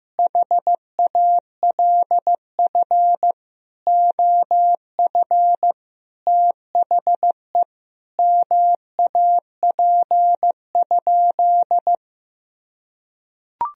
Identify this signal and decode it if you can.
Morse code, heard as a keyed tone, about 15 wpm, 700 Hz. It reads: HALF OF THE MAP?
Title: Sentences from Top 1000 Words - Review Part 2 - 15wpm